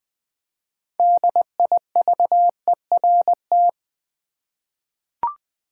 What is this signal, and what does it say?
Morse code, heard as a keyed tone, about 20 wpm, 700 Hz. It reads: DIVERT
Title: 6 Letter Words - First 3 Letters Given - 20wpm